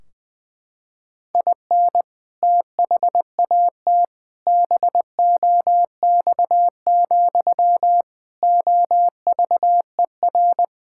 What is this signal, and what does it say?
Morse code, heard as a keyed tone, about 21 wpm, 700 Hz. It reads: IN THAT BOX, OVER
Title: Sentences from Top 1600 Words 20wpm